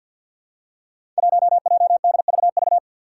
Morse code, heard as a keyed tone, about 47 wpm, 700 Hz. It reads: 1JDVV
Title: Five Challenging Letter-Number Combinations 50wpm